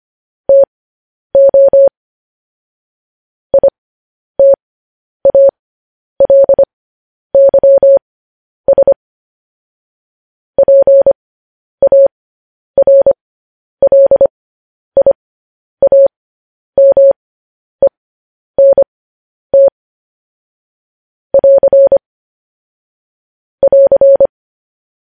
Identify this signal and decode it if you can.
Morse code, heard as a keyed tone, about 25 wpm, spaced as for 10 wpm, 550 Hz. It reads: TO ITALYS PARLIAMENT <AR> <AR>
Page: news headlines in morse code at 10 wpm